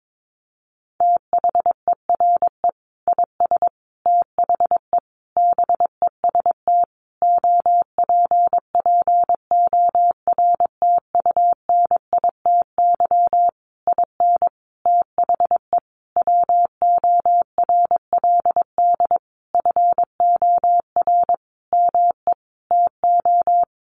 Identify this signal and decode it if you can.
Morse code, heard as a keyed tone, about 22 wpm, 700 Hz. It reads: THERE IS THE BEST OPPORTUNITY IN THE WORLD FOR ME TO
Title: Sentences from Top 1500 Words 22wpm